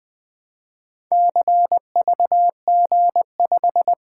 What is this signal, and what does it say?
Morse code, heard as a keyed tone, about 20 wpm, 700 Hz. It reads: CVG5
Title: Four Letter-Number Combinations - 20wpm